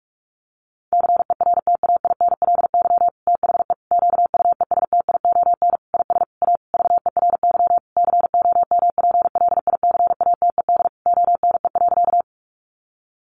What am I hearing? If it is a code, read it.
Morse code, heard as a keyed tone, about 44 wpm, 700 Hz. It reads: CERTAINLY THE QUESTION IS A VERY COMPLICATED ONE.
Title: Sentences from Top 1200 Words 45wpm